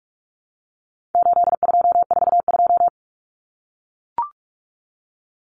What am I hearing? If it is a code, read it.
Morse code, heard as a keyed tone, about 44 wpm, 700 Hz. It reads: 8242
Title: Common Cut Numbers - 4-Digit 45wpm